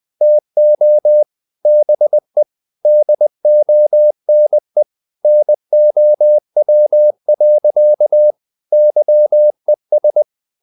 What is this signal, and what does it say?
Morse code, heard as a keyed tone, about 20 wpm, 600 Hz. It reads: TO BE DONE NOW. YES
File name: the old man and the sea part 2 20wpm